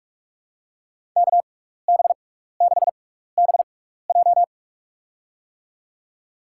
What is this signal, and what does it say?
Morse code, heard as a keyed tone, about 46 wpm, spaced as for 16 wpm, 700 Hz. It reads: KB6BJ
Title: USA Call Signs - 5 Characters - 6x Character Spacing - 45wpm